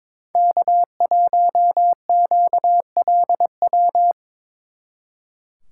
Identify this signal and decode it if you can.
Morse code, heard as a keyed tone, about 25 wpm, 700 Hz. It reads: K1QLW
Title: Speed Racing - USA 1x3 Call Signs 22wpm